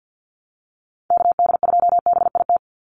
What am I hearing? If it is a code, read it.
Morse code, heard as a keyed tone, about 50 wpm, 700 Hz. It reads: KD2BIT
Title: USA Call Signs 50wpm